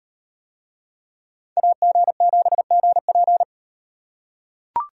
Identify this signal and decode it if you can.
Morse code, heard as a keyed tone, about 38 wpm, 700 Hz. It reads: AG7GP